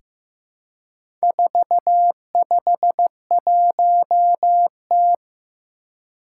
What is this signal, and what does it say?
Morse code, heard as a keyed tone, about 15 wpm, 700 Hz. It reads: 451T
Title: Four Challenging Letter-Number Combinations 15wpm